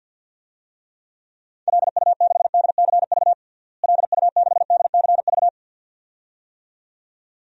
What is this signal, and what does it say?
Morse code, heard as a keyed tone, about 48 wpm, 700 Hz. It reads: LU6DCV LU6DCV
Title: DXCC-100 Call Signs - 6 Characters - Repeated 2x - 50wpm